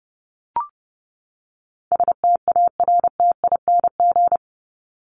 Morse code, heard as a keyed tone, about 30 wpm, 700 Hz. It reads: STARTING
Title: ING Suffix - Top 100 Words with Base Words 30wpm